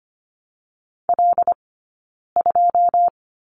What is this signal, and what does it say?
Morse code, heard as a keyed tone, about 25 wpm, 700 Hz. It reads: L 2
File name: Lesson 15d - Number 5 - Two Characters 25wpm